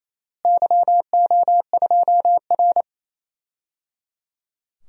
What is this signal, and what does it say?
Morse code, heard as a keyed tone, about 27 wpm, 700 Hz. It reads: YO2R
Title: DXCC-100 Call Signs - 4 Characters - Repeated 3x - 28wpm